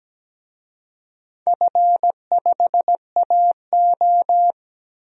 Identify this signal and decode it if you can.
Morse code, heard as a keyed tone, about 17 wpm, 700 Hz. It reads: F5AO
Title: DXCC-100 Call Signs - 4 Characters - 17wpm